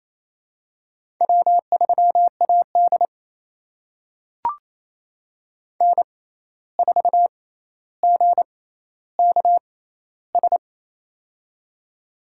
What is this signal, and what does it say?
Morse code, heard as a keyed tone, about 28 wpm, spaced as for 13 wpm, 700 Hz. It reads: W3AD N4GKS
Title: USA Call Signs - 6x Character Spacing - 28wpm